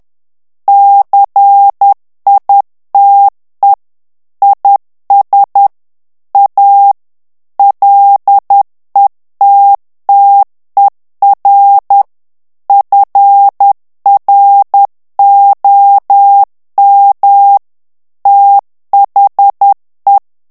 Morse code, heard as a keyed tone, about 10 wpm, 800 Hz. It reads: CITE IS A LETTER FROM THE
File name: morse mp3 / ANGLAIS texte clair 2 vitesse 10 mots minute